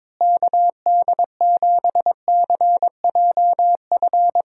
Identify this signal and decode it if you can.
Morse code, heard as a keyed tone, about 22 wpm, 700 Hz. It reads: KD7CJF
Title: USA Call Signs - 6 Characters - Repeated 3x - 22wpm